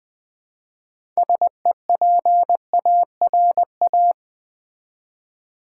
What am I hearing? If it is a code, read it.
Morse code, heard as a keyed tone, about 20 wpm, 700 Hz. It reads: SEPARA
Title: Build A Word - 2 Letters At A Time - 8-Letter Words - 20wpm